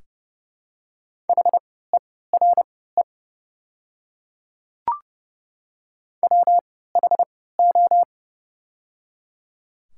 Morse code, heard as a keyed tone, about 30 wpm, spaced as for 17 wpm, 700 Hz. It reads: HERE WHO
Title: 3x Character Spacing - Top 200 Words - 30wpm